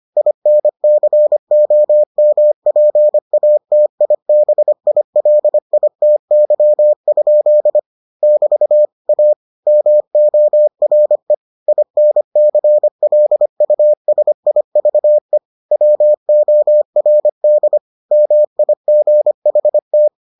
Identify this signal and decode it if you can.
Morse code, heard as a keyed tone, about 25 wpm, 600 Hz. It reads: INCOMPATIBILITY? <BT> A MORE INCLUSIVE WORD MIGHT